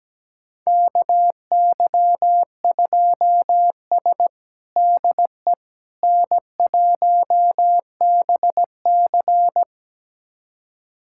Morse code, heard as a keyed tone, about 17 wpm, 700 Hz. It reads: KY2S DE N1BC
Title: USA Callsign DE Callsign - 4 Characters - 17wpm